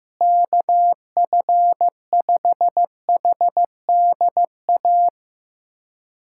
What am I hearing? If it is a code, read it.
Morse code, heard as a keyed tone, about 15 wpm, 700 Hz. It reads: KF5HDA
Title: USA Callsign DE Callsign 15wpm